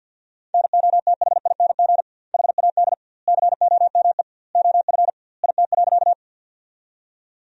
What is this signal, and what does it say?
Morse code, heard as a keyed tone, about 50 wpm, 700 Hz. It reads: NOTHING HAD COME OF IT.
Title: 2x Word-Spacing - Sentences from Top 500 Words - 50wpm